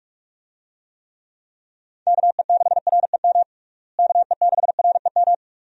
Morse code, heard as a keyed tone, about 46 wpm, 700 Hz. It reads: KE6REM KE6REM
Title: USA Call Signs - 6 Characters - Repeated 2x - 45wpm